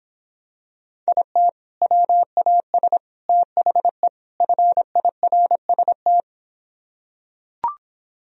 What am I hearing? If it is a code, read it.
Morse code, heard as a keyed tone, about 26 wpm, 700 Hz. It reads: IT WAS THE FIRST